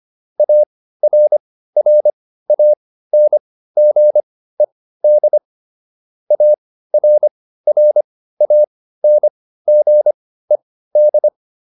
Morse code, heard as a keyed tone, about 25 wpm, spaced as for 15 wpm, 600 Hz. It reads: ARRANGED ARRANGED